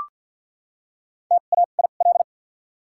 Morse code, heard as a keyed tone, about 48 wpm, spaced as for 35 wpm, 700 Hz. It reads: TAIL
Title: 2x Character Spacing - Top 500 Words - 50wpm